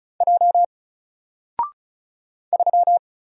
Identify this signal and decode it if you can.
Morse code, heard as a keyed tone, about 36 wpm, 700 Hz. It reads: J 3
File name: Single-Letter-Number-35wpm